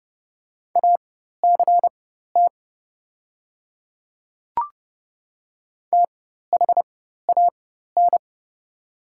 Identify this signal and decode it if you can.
Morse code, heard as a keyed tone, about 30 wpm, spaced as for 14 wpm, 700 Hz. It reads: ACT THAN
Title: 4x Character Spacing - Top 200 Words - 30wpm